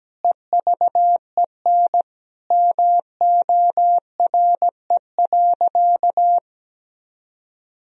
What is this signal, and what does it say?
Morse code, heard as a keyed tone, about 17 wpm, 700 Hz. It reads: EVEN MORE.
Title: Sentences from Top 1000 Words - Review Part 1 - 17wpm